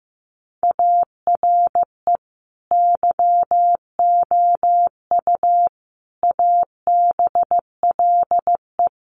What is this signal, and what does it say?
Morse code, heard as a keyed tone, about 15 wpm, 700 Hz. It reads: ARE YOU ABLE
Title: Sentences from Top 1500 Words 15wpm